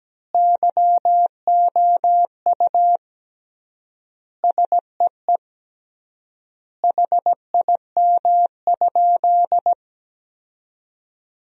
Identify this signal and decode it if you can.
Morse code, heard as a keyed tone, about 17 wpm, 700 Hz. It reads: YOU SEE HIM?
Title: 3x Word-Spacing - Sentences from Top 100 Words - 17wpm